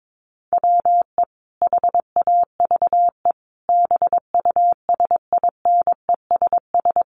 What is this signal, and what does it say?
Morse code, heard as a keyed tone, about 22 wpm, 700 Hz. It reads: WE HAVE BUSINESS
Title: Sentences from Top 600 Words 22wpm